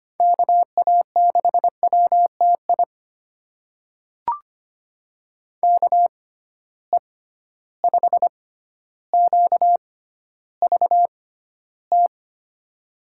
Morse code, heard as a keyed tone, about 25 wpm, spaced as for 13 wpm, 700 Hz. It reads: KA6WTI KE5QVT
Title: USA Call Signs - 6 Characters - 6x Character Spacing - 25wpm